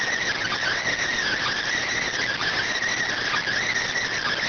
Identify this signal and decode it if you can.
Emergency Alert System (SAME) tones, heard as harsh buzzing bursts. It is heard from 0.3 to 3.8 s.